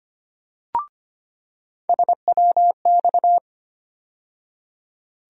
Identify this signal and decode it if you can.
Morse code, heard as a keyed tone, about 25 wpm, 700 Hz. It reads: SWX
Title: USA Call Sign Postfixes - 3 Letters 25wpm